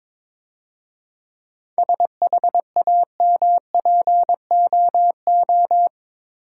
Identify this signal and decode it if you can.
Morse code, heard as a keyed tone, about 24 wpm, 690 Hz. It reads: SHAMPOO
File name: Easy Words - Speed-Racing with Hints - 22wpm